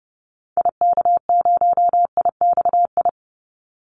Morse code, heard as a keyed tone, about 30 wpm, 700 Hz. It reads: IK0IXI
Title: Call Signs Repeated 3x - 30wpm